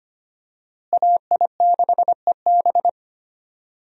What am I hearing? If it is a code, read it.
Morse code, heard as a keyed tone, about 25 wpm, 700 Hz. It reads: AI6EB
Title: USA Call Signs - 5 Characters - Repeated 3x - 25wpm